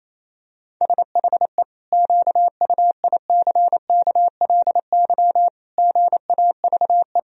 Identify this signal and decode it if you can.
Morse code, heard as a keyed tone, about 28 wpm, 700 Hz. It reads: SHE QUICKLY GAVE
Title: Sentences from Top 1000 Words - Review Part 4 - 28wpm